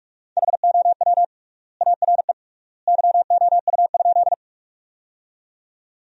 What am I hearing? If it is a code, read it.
Morse code, heard as a keyed tone, about 46 wpm, 700 Hz. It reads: HOW ARE YOU?
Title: 3x Word-Spacing - Sentences from Top 100 Words - 45wpm